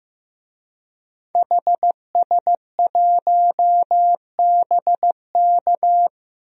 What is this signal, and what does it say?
Morse code, heard as a keyed tone, about 15 wpm, 700 Hz. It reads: HS1BK
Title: DXCC-100 Call Signs - 5 Characters - 15wpm